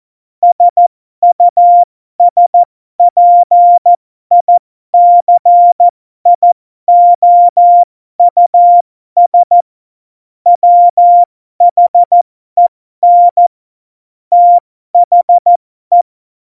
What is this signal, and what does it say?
Morse code, heard as a keyed tone, about 14 wpm, spaced as for 12 wpm, 700 Hz. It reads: SUSPICIOUS WHEN THE
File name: Morsekurs / Witze